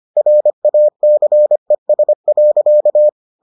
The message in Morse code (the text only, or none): RACES.